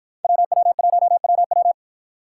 Morse code, heard as a keyed tone, about 50 wpm, 700 Hz. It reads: WW1WW